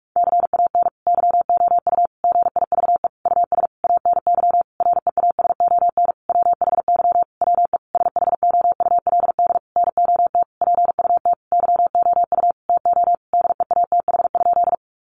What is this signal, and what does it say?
Morse code, heard as a keyed tone, about 44 wpm, 700 Hz. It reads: CAN YOU GIVE US ANY REASON WHY WE SHOULD NOT PUT YOU TO DEATH?
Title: Sentences from Top 1100 Words 45wpm